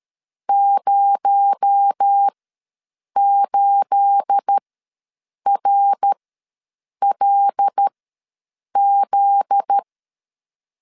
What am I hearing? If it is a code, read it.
Morse code, heard as a keyed tone, about 13 wpm, spaced as for 7 wpm, 800 Hz. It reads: 08RLZ
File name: LEZIONI / 9 2867 ws 12wpm 3431